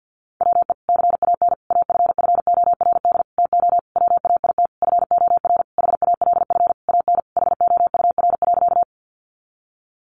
Morse code, heard as a keyed tone, about 50 wpm, 700 Hz. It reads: WE CAN AFFORD TO WAIT FOR HALF AN HOUR.
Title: Sentences from Top 1100 Words 50wpm